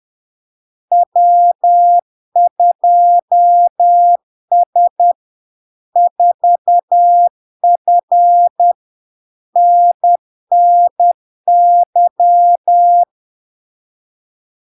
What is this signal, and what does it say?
Morse code, heard as a keyed tone, about 10 wpm, 700 Hz. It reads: W2S 4F NNY
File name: LARK-FD-2022-Log-in-CW-10wpm-700-Hz